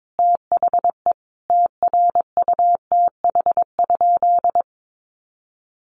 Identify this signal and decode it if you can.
Morse code, heard as a keyed tone, about 22 wpm, 700 Hz. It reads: THE TRUTH?
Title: Sentences from Top 800 Words 22wpm